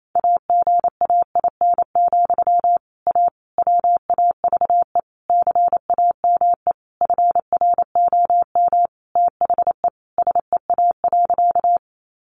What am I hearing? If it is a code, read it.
Morse code, heard as a keyed tone, about 28 wpm, 700 Hz. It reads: AGAIN, A WAVE CAME FROM THE SEA.
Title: Sentences from Top 1400 Words 28wpm